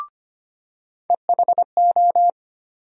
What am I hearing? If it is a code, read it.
Morse code, heard as a keyed tone, about 25 wpm, 700 Hz. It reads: EHO
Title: USA Call Sign Postfixes - 3 Letters 25wpm